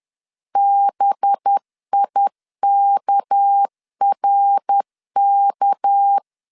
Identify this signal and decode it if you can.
Morse code, heard as a keyed tone, about 10 wpm, 800 Hz. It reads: BIKRK
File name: LEZIONI / 6 PKX0 / 6 PK 10wpm 2436